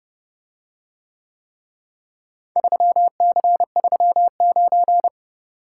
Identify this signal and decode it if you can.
Morse code, heard as a keyed tone, about 30 wpm, 700 Hz. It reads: 3C39